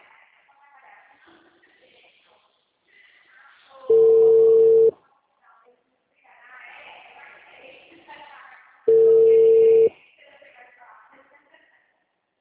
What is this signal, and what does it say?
Ringback tone (China, 450 Hz), with speech.
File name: out- -2017- - - .37679